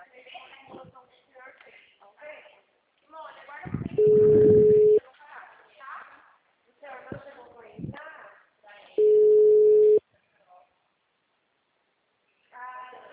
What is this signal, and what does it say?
Ringback tone (Europe, 425 Hz), with speech.
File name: out- -2029- - - .43870